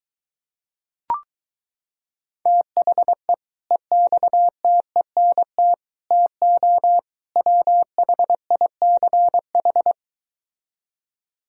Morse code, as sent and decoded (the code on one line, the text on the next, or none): - .... / . -..- - . -. - / - --- / .-- .... .. -.-. ....
THE EXTENT TO WHICH